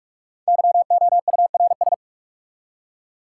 Morse code, heard as a keyed tone, about 46 wpm, 700 Hz. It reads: YOURS
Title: Five Letter Words - 45wpm